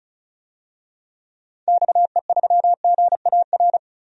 Morse code, heard as a keyed tone, about 34 wpm, 700 Hz. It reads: XE3GAR
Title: DXCC-100 Call Signs - 6 Characters - 35wpm